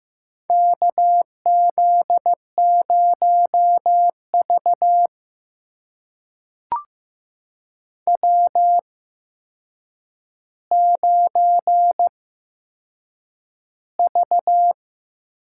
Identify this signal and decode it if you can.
Morse code, heard as a keyed tone, about 15 wpm, spaced as for 7 wpm, 700 Hz. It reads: KZ0V W9V
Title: USA Call Signs - 8x Character Spacing - 15wpm